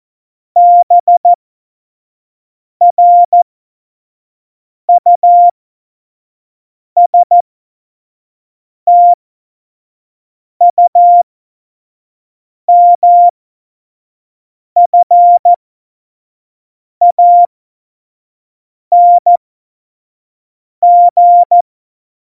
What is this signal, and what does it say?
Morse code, heard as a keyed tone, about 14 wpm, spaced as for 5 wpm, 700 Hz. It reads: BRUSTUMFANG
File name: Morsekurs / Witze / Witz 18. Februar 2020, 25 F70